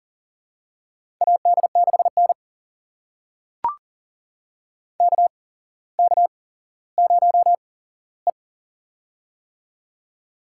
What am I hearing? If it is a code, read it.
Morse code, heard as a keyed tone, about 37 wpm, spaced as for 17 wpm, 700 Hz. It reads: AD6N KK0E
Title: USA Call Signs - 4 Characters - 8x Character Spacing - 40wpm